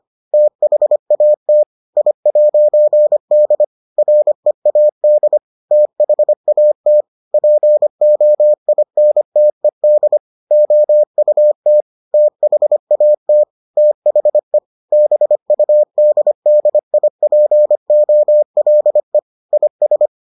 Morse code, heard as a keyed tone, about 25 wpm, 600 Hz. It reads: THAT I'D READ THAT POINTED OUT THAT THE BUDDIPOLE IS